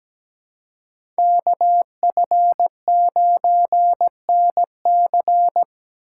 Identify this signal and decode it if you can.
Morse code, heard as a keyed tone, about 17 wpm, 700 Hz. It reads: KF9NC
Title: USA Call Signs - 5 Characters - 3x Character Spacing - 17wpm